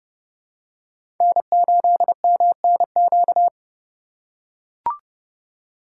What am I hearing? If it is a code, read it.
Morse code, heard as a keyed tone, about 30 wpm, 700 Hz. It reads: N8MNQ
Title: USA Call Signs - 5 Characters - 2x Character Spacing - 30wpm